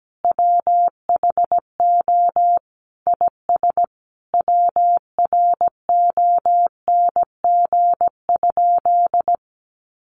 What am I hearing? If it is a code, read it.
Morse code, heard as a keyed tone, about 17 wpm, 700 Hz. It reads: WHO IS WRONG?
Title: Sentences from Top 900 Words 17wpm